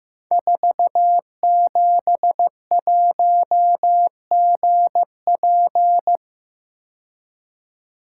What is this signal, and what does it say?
Morse code, heard as a keyed tone, about 15 wpm, 700 Hz. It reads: 471GP
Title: Code Groups - 5 Letter:Numbers - 15wpm